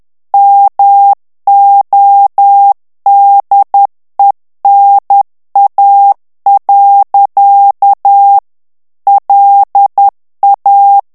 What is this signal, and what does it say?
Morse code, heard as a keyed tone, about 10 wpm, 800 Hz. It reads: MODENA. LA
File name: ITALIEN texte clair 1 vitesse 10 mots minute